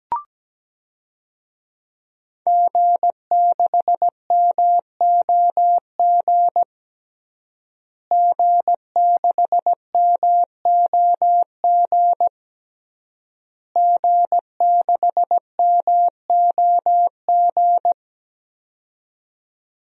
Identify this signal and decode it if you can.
Morse code, heard as a keyed tone, about 17 wpm, 700 Hz. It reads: G6MOG G6MOG G6MOG
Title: DXCC-100 Call Signs - 5 Characters - Repeated 3x - 17wpm